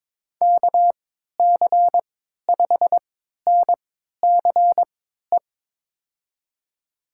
Morse code, heard as a keyed tone, about 21 wpm, spaced as for 12 wpm, 700 Hz. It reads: KC5NCE
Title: USA Call Signs - 6 Characters - 3x Character Spacing - 22wpm